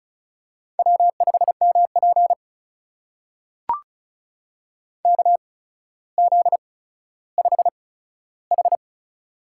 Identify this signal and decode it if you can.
Morse code, heard as a keyed tone, about 36 wpm, spaced as for 15 wpm, 700 Hz. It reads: W5MP KZ5H
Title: USA Call Signs - 4 Characters - 8x Character Spacing - 35wpm